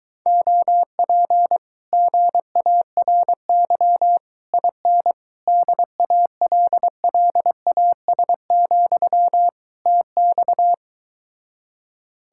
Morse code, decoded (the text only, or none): OP GARY IN DALLAS, TX